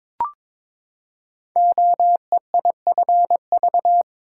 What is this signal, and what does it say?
Morse code, heard as a keyed tone, about 22 wpm, 700 Hz. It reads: OEIFV
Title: Code Groups - 5 Letters - 22wpm